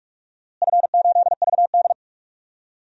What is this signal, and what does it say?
Morse code, heard as a keyed tone, about 46 wpm, 700 Hz. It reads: F8VD
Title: DXCC-100 Call Signs - 4 Characters - Repeated 3x - 45wpm